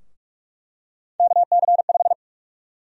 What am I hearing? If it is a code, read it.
Morse code, heard as a keyed tone, about 46 wpm, 700 Hz. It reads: KC5
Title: USA Call Sign Prefixes - Repeated 2x - 45wpm